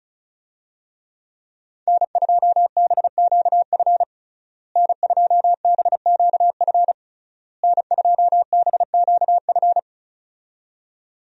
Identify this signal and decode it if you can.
Morse code, heard as a keyed tone, about 34 wpm, 700 Hz. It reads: N2BQF N2BQF N2BQF
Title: USA Call Signs - 5 Characters - Repeated 3x - 35wpm